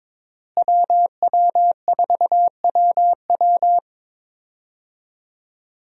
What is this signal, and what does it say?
Morse code, heard as a keyed tone, about 22 wpm, 700 Hz. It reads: WW4WW
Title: USA Call Signs - 5 Characters - Repeated 2x - 22wpm